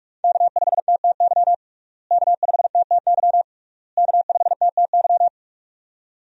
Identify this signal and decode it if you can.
Morse code, heard as a keyed tone, about 46 wpm, 700 Hz. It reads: K5TTY K5TTY K5TTY
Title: USA Call Signs - 5 Characters - Repeated 3x - 45wpm